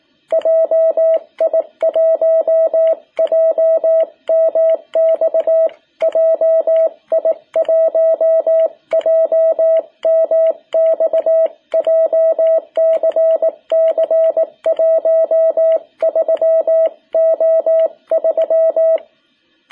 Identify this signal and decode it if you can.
Morse code, heard as a keyed tone, about 18 wpm, 660 Hz. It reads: JI1JMX JI1JMX JCC 13O3
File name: jh8cbh / 2021cw